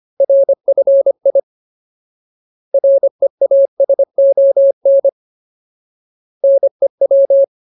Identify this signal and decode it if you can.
Morse code, heard as a keyed tone, about 25 wpm, 550 Hz. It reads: RFI REASON NEW
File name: Ultimate Word List / 25WPM Ultimate